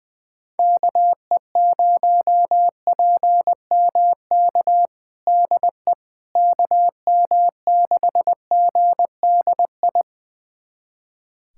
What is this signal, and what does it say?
Morse code, heard as a keyed tone, about 20 wpm, 700 Hz. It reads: KE0PMK DE KM6GDI
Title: USA Callsign DE Callsign - 6 Characters - 20wpm